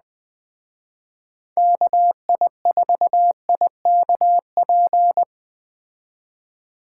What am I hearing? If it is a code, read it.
Morse code, heard as a keyed tone, about 20 wpm, 700 Hz. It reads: KI4IKP
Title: Speed Racing - USA 2x3 Call Signs 17wpm